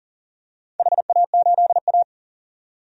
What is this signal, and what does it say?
Morse code, heard as a keyed tone, about 40 wpm, 700 Hz. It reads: HA8A
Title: DXCC-100 Call Signs - 4 Characters - 40wpm